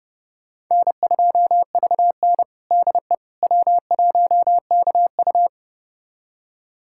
Morse code, heard as a keyed tone, about 30 wpm, 700 Hz. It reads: N2VN DE W1KU
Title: USA Callsign DE Callsign 30wpm